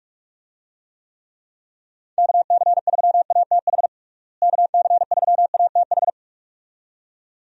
Morse code, heard as a keyed tone, about 41 wpm, 700 Hz. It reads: KC3ATH KC3ATH
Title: USA Call Signs - 6 Characters - Repeated 2x - 45wpm